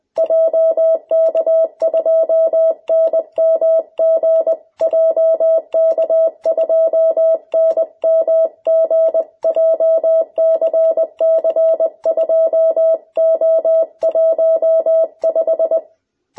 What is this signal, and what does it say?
Morse code, heard as a keyed tone, about 20 wpm, 660 Hz. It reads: JK2NMG JK2NMG JCC2O15
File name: jh8cbh / 2021cw